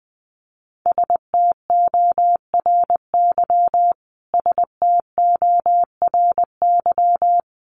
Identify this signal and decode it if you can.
Morse code, heard as a keyed tone, about 20 wpm, 700 Hz. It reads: STORY STORY
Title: Top 500 Words Repeated 20wpm